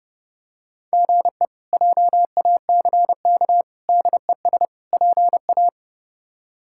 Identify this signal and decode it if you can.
Morse code, heard as a keyed tone, about 30 wpm, 700 Hz. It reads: GE JACK DES PA
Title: K1USN SST QSO Practice - 30wpm